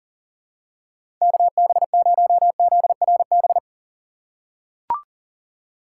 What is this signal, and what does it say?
Morse code, heard as a keyed tone, about 40 wpm, 700 Hz. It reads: KB0ZRB
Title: USA Call Signs - 6 Characters - 3x Character Spacing - 40wpm